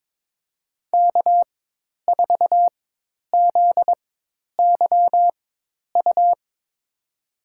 Morse code, heard as a keyed tone, about 22 wpm, spaced as for 10 wpm, 700 Hz. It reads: K4ZYU
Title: USA Call Signs - 4x Character Spacing - 22wpm